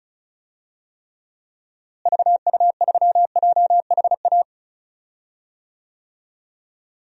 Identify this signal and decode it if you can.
Morse code, heard as a keyed tone, about 36 wpm, 700 Hz. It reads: VU3JHA
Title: DXCC-100 Call Signs - 6 Characters - Repeated 2x - 35wpm